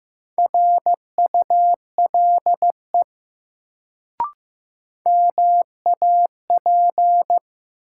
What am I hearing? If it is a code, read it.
Morse code, heard as a keyed tone, about 15 wpm, 700 Hz. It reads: RULE MAP
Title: Sentences from Top 400 Words - New Word Drill - 15wpm